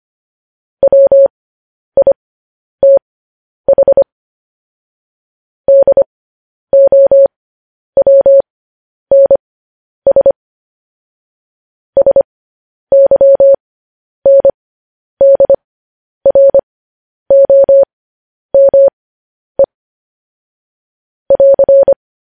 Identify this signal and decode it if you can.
Morse code, heard as a keyed tone, about 25 wpm, spaced as for 10 wpm, 550 Hz. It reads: WITH DOWNS SYNDROME <AR>
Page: News Headlines in Morse Code at 10 WPM